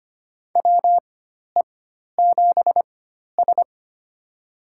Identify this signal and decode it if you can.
Morse code, heard as a keyed tone, about 25 wpm, spaced as for 12 wpm, 700 Hz. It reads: WE7S
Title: USA Call Signs - 4x Character Spacing - 25wpm